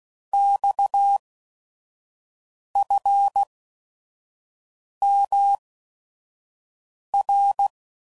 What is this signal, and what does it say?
Morse code, heard as a keyed tone, about 16 wpm, spaced as for 5 wpm, 800 Hz. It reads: XFMR
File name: CW Code Groups @5WPM / Technical Subjects / SwPwrSup